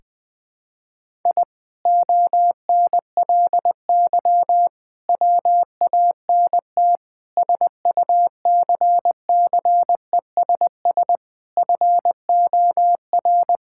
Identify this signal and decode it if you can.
Morse code, heard as a keyed tone, about 20 wpm, 700 Hz. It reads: I ONLY WANT SUCCESS FOR
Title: Sentences from Top 1000 Words - Review Part 6 - 20wpm